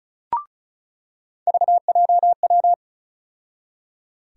Morse code, heard as a keyed tone, about 36 wpm, 700 Hz. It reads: VJW
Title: USA Call Sign Postfixes - 3 Letters 35wpm